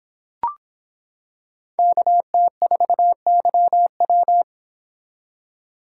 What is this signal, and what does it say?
Morse code, heard as a keyed tone, about 25 wpm, 700 Hz. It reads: KT4YW